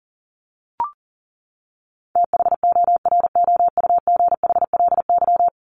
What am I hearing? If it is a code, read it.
Morse code, heard as a keyed tone, about 40 wpm, 700 Hz. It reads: THOROUGHLY
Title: LY Suffix - Top 100 Words with Base Words 40wpm